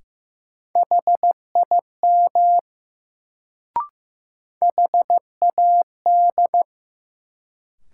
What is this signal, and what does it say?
Morse code, heard as a keyed tone, about 15 wpm, 700 Hz. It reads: HIM HAD
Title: Sets of 2 Words - New Word Drill - 15wpm